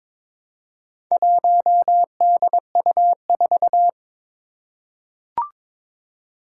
Morse code, heard as a keyed tone, about 23 wpm, 700 Hz. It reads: 1DU4
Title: Four Challenging Letter-Number Combinations 22wpm